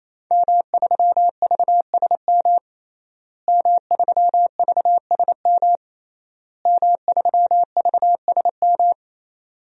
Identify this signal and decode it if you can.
Morse code, heard as a keyed tone, about 27 wpm, 700 Hz. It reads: M3VSM M3VSM M3VSM
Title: DXCC-100 Call Signs - 5 Characters - Repeated 3x - 28wpm